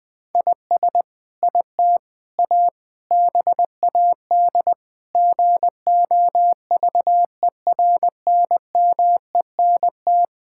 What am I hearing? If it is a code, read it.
Morse code, heard as a keyed tone, about 21 wpm, 700 Hz. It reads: IS IT A BAD GOVERNMENT
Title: Sentences from Top 1000 Words - Review Part 5 - 20wpm